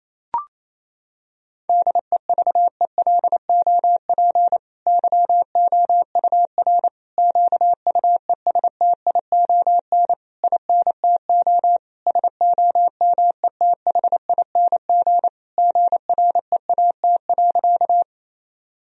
Morse code, heard as a keyed tone, about 28 wpm, 700 Hz. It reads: DEVELOP YOUR QUESTION INTO SOMETHING GREAT.
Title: Sentences from Top 1000 Words - Review Part 2 - 28wpm